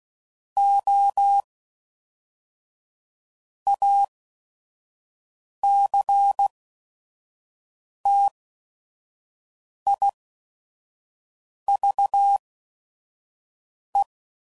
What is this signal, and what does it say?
Morse code, heard as a keyed tone, about 16 wpm, spaced as for 5 wpm, 800 Hz. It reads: O ACTIVE
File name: CW Code Groups @5WPM / Technical Subjects / AND Gate